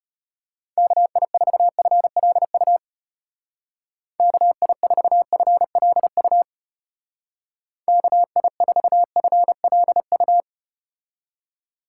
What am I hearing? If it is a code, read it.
Morse code, heard as a keyed tone, about 33 wpm, 700 Hz. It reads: KI4FLU KI4FLU KI4FLU